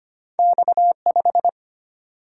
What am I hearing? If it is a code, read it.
Morse code, heard as a keyed tone, about 25 wpm, 700 Hz. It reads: X5
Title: Single-Letter Single-Number 25wpm